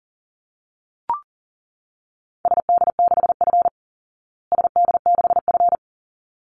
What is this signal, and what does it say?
Morse code, heard as a keyed tone, about 40 wpm, 700 Hz. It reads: SD6F SD6F
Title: Call Signs Repeated 3x - 40wpm